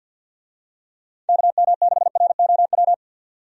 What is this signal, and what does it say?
Morse code, heard as a keyed tone, about 48 wpm, 700 Hz. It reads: KM6ROW
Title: USA Call Signs - 6 Characters - 50wpm